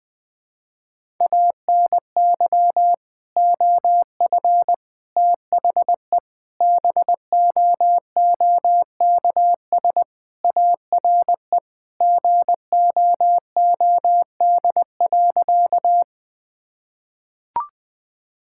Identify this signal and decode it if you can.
Morse code, heard as a keyed tone, about 21 wpm, 700 Hz. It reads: ANY OF THE BOOKS ARE GOOD.
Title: Sentences with 3-Word Phrases - 20wpm